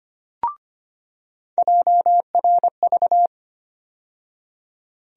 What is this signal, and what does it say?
Morse code, heard as a keyed tone, about 25 wpm, 700 Hz. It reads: JRV
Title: USA Call Sign Postfixes - 3 Letters 25wpm